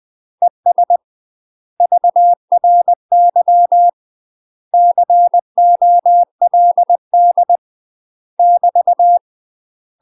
Morse code, heard as a keyed tone, about 20 wpm, 700 Hz. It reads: ES VRY COLD <BT>
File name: UK MASTER QSOs TRAINING / 20WPM / UK MASTER QSOs 0015